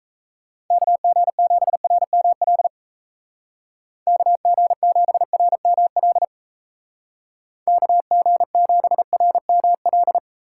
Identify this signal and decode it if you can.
Morse code, heard as a keyed tone, about 34 wpm, 700 Hz. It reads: KG7RML KG7RML KG7RML
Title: Speed Racing - USA 2x3 Call Signs 28wpm